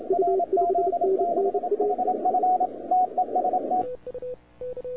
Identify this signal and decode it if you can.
Morse code, heard as a keyed tone, about 28 wpm, 690 Hz, with radio noise. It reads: RZ6HF TEST
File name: wwcw08 / k5zd wwcw08 0200z